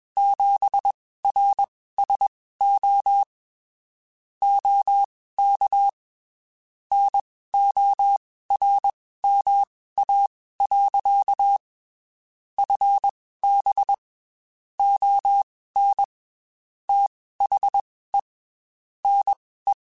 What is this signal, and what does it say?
Morse code, heard as a keyed tone, about 21 wpm, spaced as for 15 wpm, 800 Hz. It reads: 7RSO OK NORMA. FB ON THE NE
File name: QSOs / QSOs-mp3s / WD0FBT 1@20